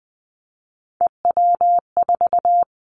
Morse code, heard as a keyed tone, about 19 wpm, 700 Hz. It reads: EW4
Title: Callsign Prefixes for Top 100 DXCC - 20wpm